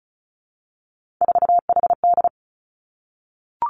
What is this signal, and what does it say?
Morse code, heard as a keyed tone, about 36 wpm, 700 Hz. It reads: 4HD
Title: Three Challenging Letter-Number Combinations 35wpm